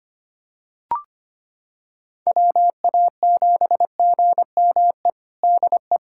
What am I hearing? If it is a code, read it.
Morse code, heard as a keyed tone, about 25 wpm, 700 Hz. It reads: WA7GME DE
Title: USA Callsign DE Callsign 25wpm